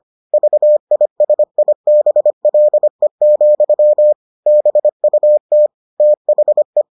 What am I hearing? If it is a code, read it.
Morse code, heard as a keyed tone, about 25 wpm, 600 Hz. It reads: VISIBLE, BUT THE